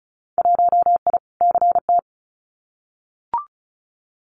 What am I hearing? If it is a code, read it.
Morse code, heard as a keyed tone, about 34 wpm, 700 Hz. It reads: 1I CT
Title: Winter Field Day Exchanges 35wpm